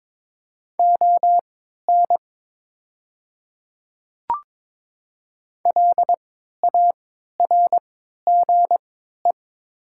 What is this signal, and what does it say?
Morse code, heard as a keyed tone, about 22 wpm, spaced as for 12 wpm, 700 Hz. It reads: ON LARGE
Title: 3x Character Spacing - Top 200 Words - 22wpm